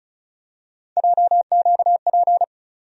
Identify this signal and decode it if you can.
Morse code, heard as a keyed tone, about 36 wpm, 700 Hz. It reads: JQP
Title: USA Call Sign Postfixes - 3 Letters 35wpm